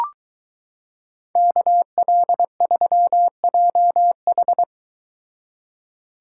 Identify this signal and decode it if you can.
Morse code, heard as a keyed tone, about 22 wpm, 700 Hz. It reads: KL3JH